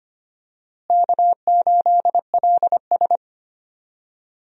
Morse code, heard as a keyed tone, about 25 wpm, 700 Hz. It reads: K8LS
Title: USA Call Signs - 4 Characters - Repeated 2x - 25wpm